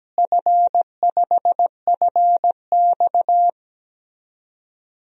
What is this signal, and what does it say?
Morse code, heard as a keyed tone, about 17 wpm, 700 Hz. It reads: F5FX
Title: DXCC-100 Call Signs - 4 Characters - Repeated 3x - 17wpm